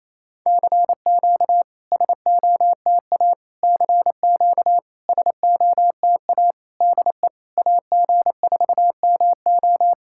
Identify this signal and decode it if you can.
Morse code, heard as a keyed tone, about 28 wpm, 700 Hz. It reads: CQ SOTA CQ SOTA DE AG4MO
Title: SOTA QSO Practice 28wpm